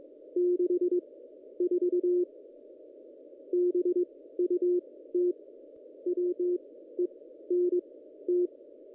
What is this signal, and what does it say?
Morse code, heard as a keyed tone, about 22 wpm, spaced as for 13 wpm, 360 Hz. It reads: 64 BUT WENT
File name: contests / ss-cw-k5tr-op-n5rz -1700